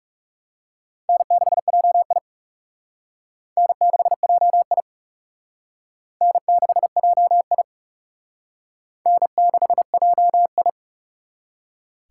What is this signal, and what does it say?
Morse code, heard as a keyed tone, about 40 wpm, 700 Hz. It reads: N6JI N6JI N6JI N6JI